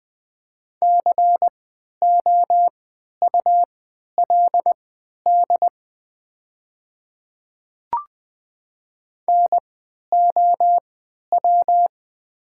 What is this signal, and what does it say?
Morse code, heard as a keyed tone, about 20 wpm, spaced as for 11 wpm, 700 Hz. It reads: COULD NOW
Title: 3x Character Spacing - Top 100 Words - 20wpm